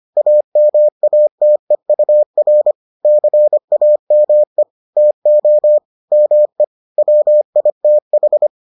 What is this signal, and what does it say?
Morse code, heard as a keyed tone, about 25 wpm, 600 Hz. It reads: AMATEUR CAME TO ME WITH